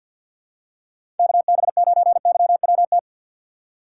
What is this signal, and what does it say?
Morse code, heard as a keyed tone, about 47 wpm, 700 Hz. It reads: KB9YWT
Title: USA Call Signs - 6 Characters - 4x Character Spacing - 50wpm